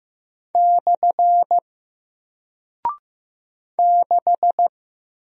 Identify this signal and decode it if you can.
Morse code, heard as a keyed tone, about 15 wpm, 700 Hz. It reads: / 6
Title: Single Character (Letter:Number:Punctuation:Prosign) Alt 15wpm